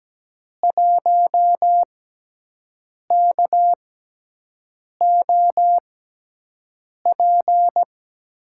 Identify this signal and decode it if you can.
Morse code, heard as a keyed tone, about 17 wpm, spaced as for 6 wpm, 700 Hz. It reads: 1KOP
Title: USA Call Signs - 6x Character Spacing - 17wpm